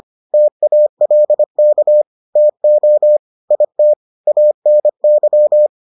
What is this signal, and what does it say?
Morse code, heard as a keyed tone, about 25 wpm, 600 Hz. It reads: TALK TO IT ANY